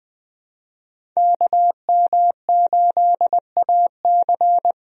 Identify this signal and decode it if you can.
Morse code, heard as a keyed tone, about 21 wpm, 700 Hz. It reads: KM8AC
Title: USA Call Signs - 5 Characters - Repeated 3x - 20wpm